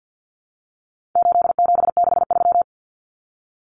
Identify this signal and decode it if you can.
Morse code, heard as a keyed tone, about 50 wpm, 700 Hz. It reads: 8763
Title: Common Cut Numbers - 4-Digit 50wpm